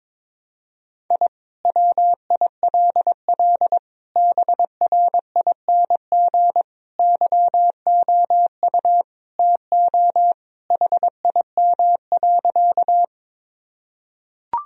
Morse code, heard as a keyed tone, about 22 wpm, 700 Hz. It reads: I WILL BRING YOU TO HIM.
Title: Sentences from Top 1000 Words - Review Part 3 - 22wpm